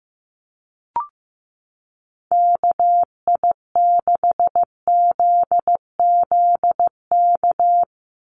Morse code, heard as a keyed tone, about 15 wpm, 700 Hz. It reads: KI6ZZK
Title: USA Call Signs 15wpm